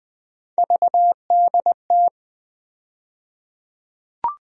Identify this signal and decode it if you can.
Morse code, heard as a keyed tone, about 20 wpm, 700 Hz. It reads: VDT
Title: USA Call Sign Postfixes - 3 Letters 20wpm